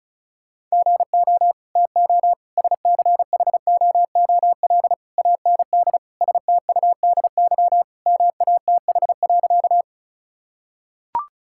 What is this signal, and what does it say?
Morse code, heard as a keyed tone, about 36 wpm, 700 Hz. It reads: GO TO SCHOOL AND STUDY MATH.
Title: Sentences from Top 1000 Words - Review Part 6 - 35wpm